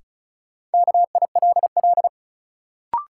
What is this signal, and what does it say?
Morse code, heard as a keyed tone, about 36 wpm, 700 Hz. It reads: KILL
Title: Sentences from Top 1400 Words - New Word Drill Alt - 35wpm